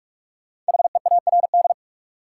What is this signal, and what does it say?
Morse code, heard as a keyed tone, about 45 wpm, 700 Hz. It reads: HEARD
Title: Sentences from Top 400 Words - New Word Drill - 45wpm